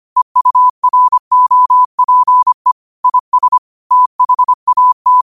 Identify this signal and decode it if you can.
Morse code, heard as a keyed tone, about 25 wpm, 1000 Hz. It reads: EUROPE IS THAT
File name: qotdfiles / 25wpm